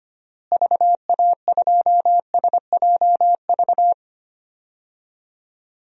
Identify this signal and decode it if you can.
Morse code, heard as a keyed tone, about 25 wpm, 700 Hz. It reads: VA2SJV
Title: DXCC-100 Call Signs - 6 Characters - Repeated 2x - 25wpm